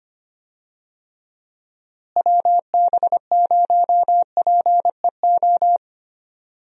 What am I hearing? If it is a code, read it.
Morse code, heard as a keyed tone, about 25 wpm, 700 Hz. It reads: WB0PEO
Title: USA Call Signs - 6 Characters - Repeated 2x - 25wpm